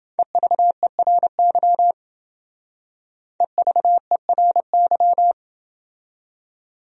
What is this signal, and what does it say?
Morse code, heard as a keyed tone, about 30 wpm, 700 Hz. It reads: EVERY EVERY